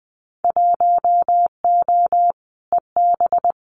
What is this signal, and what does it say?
Morse code, heard as a keyed tone, about 20 wpm, 700 Hz. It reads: 1O EB
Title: Winter Field Day Exchanges 20wpm